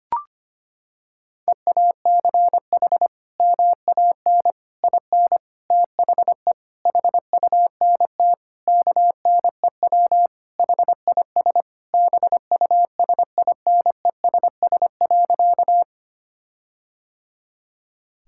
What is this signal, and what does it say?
Morse code, heard as a keyed tone, about 25 wpm, 700 Hz. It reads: EACH MAN IN THE HUNT KNEW HIS BUSINESS.
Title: Sentences from Top 1600 Words 25wpm